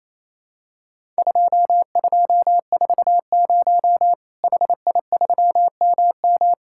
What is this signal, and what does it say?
Morse code, heard as a keyed tone, about 27 wpm, 700 Hz. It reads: 2240 HI3MM
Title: Worked All Europe DX Contest - QTC Exchange - 28wpm